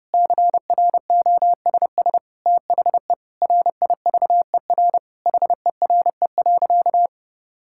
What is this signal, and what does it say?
Morse code, heard as a keyed tone, about 32 wpm, 700 Hz. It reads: CROSS THE RIVER HERE.
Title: Sentences from Top 1000 Words - Review Part 5 - 30wpm